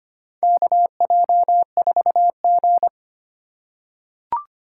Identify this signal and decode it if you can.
Morse code, heard as a keyed tone, about 25 wpm, 700 Hz. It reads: KJ4G
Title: USA Call Signs - 4 Characters - 3x Character Spacing - 25wpm